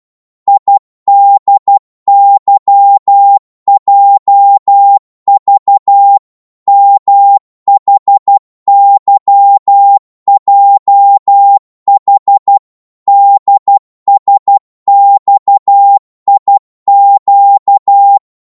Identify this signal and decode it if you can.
Morse code, heard as a keyed tone, about 12 wpm, 800 Hz. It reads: IDYJV MHYJH DSXIQ
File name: Morse / ltr-12wpm